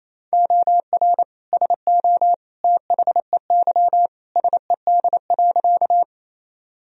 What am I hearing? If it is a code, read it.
Morse code, heard as a keyed tone, about 28 wpm, 700 Hz. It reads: OR SO THEY SED.
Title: Sentences from Top 100 Words and Abbreviations 28wpm